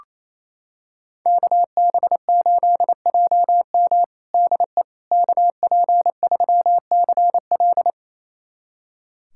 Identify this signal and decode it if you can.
Morse code, heard as a keyed tone, about 27 wpm, 700 Hz. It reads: KB8JM DE KP3CL
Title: USA Callsign DE Callsign - 5 Characters - 28wpm